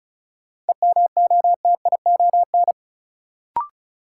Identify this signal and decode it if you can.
Morse code, heard as a keyed tone, about 36 wpm, 700 Hz. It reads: EMOTION
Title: Sentences from Top 1600 Words - New Word Drill Alt - 35wpm